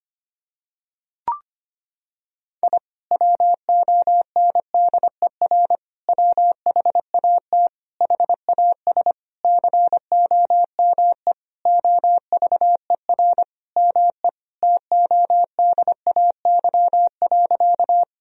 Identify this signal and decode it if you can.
Morse code, heard as a keyed tone, about 25 wpm, 700 Hz. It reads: I WONDER WHAT HAS COME OVER ME TODAY.
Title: Sentences from Top 1000 Words - Review Part 4 - 25wpm